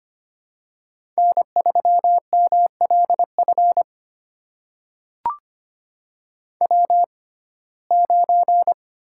Morse code, heard as a keyed tone, about 25 wpm, spaced as for 18 wpm, 700 Hz. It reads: N3MLF W9
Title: USA Call Signs - 5 Characters - 6x Character Spacing - 25wpm